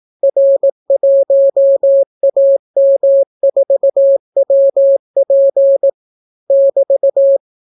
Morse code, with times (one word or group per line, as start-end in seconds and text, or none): 0.2-5.9 R1AM4WP
6.5-7.4 <BT>